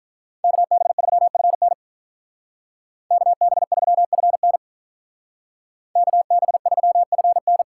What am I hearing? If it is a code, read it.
Morse code, heard as a keyed tone, about 46 wpm, 700 Hz. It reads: KB3FN KB3FN KB3FN